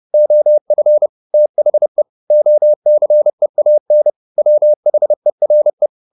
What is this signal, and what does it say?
Morse code, heard as a keyed tone, about 30 wpm, 600 Hz. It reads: OF THE OCEAN WHERE
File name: the old man and the sea / the old man and the sea 1 30wpm